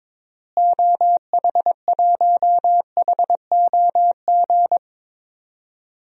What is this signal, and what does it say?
Morse code, heard as a keyed tone, about 22 wpm, 700 Hz. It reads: OH1HOG
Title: DXCC-100 Call Signs - 6 Characters - Repeated 2x - 22wpm